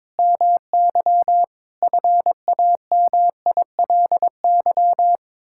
Morse code, heard as a keyed tone, about 22 wpm, 700 Hz. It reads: MY FAMILY
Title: Sentences from Top 1000 Words - Review Part 6 - 22wpm